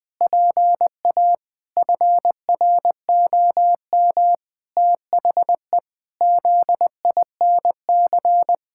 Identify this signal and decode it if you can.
Morse code, heard as a keyed tone, about 21 wpm, 700 Hz. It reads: PA FROM THE ZINC
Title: Pangram Challenge v2 - 20wpm